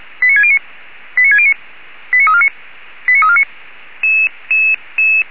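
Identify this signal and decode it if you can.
Five-tone selective call (ZVEI), heard as quick runs of five tones, tones of about 70 ms. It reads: 89609 89609 89349 89349
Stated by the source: ZVEI five-tone selective call, received on SDR.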